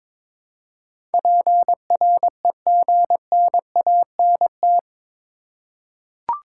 Morse code, heard as a keyed tone, about 22 wpm, 700 Hz. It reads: PREGNANT
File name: practice / 22wpm / Eight Letter Words - 22wpm